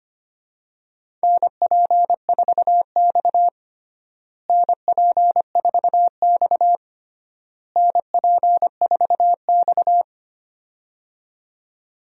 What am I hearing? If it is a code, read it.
Morse code, heard as a keyed tone, about 25 wpm, 700 Hz. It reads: NP4X NP4X NP4X
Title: USA Call Signs - 4 Characters - Repeated 3x - 25wpm